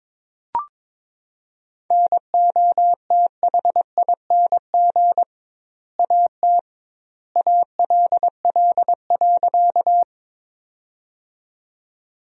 Morse code, heard as a keyed tone, about 22 wpm, 700 Hz. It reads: NOTHING AT ALL.
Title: 2x Word-Spacing - Sentences from Top 500 Words - 22wpm